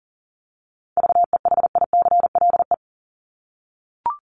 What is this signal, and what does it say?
Morse code, heard as a keyed tone, about 40 wpm, 700 Hz. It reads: VEHICLE
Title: Top 1751-2000 Words 40wpm